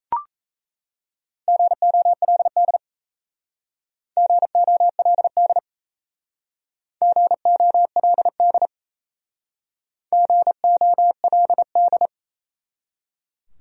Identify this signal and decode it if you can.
Morse code, heard as a keyed tone, about 36 wpm, 700 Hz. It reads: GOLD GOLD GOLD GOLD